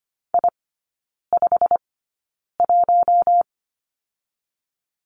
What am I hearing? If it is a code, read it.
Morse code, heard as a keyed tone, about 25 wpm, spaced as for 9 wpm, 700 Hz. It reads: I51
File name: Lesson 15e - Number 5 - Three Characters 25wpm